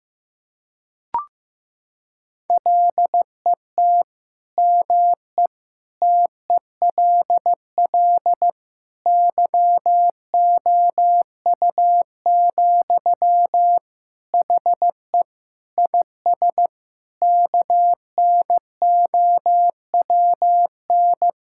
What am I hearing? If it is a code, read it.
Morse code, heard as a keyed tone, about 15 wpm, 700 Hz. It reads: LET ME TELL YOU, HE IS KNOWN
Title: Sentences from Top 1600 Words 15wpm